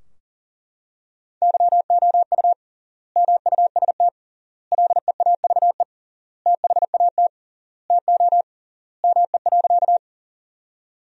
Morse code, heard as a keyed tone, about 37 wpm, 700 Hz. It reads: YOU MUST LEAVE THAT TO ME.
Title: 3x Word-Spacing - Sentences from Top 400 Words - 40wpm